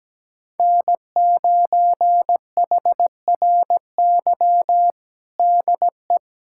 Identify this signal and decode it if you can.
Morse code, heard as a keyed tone, about 17 wpm, 700 Hz. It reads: N9HRY DE
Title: USA Callsign DE Callsign - 5 Characters - 17wpm